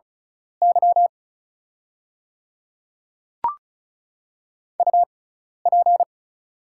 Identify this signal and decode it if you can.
Morse code, heard as a keyed tone, about 36 wpm, spaced as for 12 wpm, 700 Hz. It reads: Y UP
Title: 6x Character Spacing - Top 100 Words - 35wpm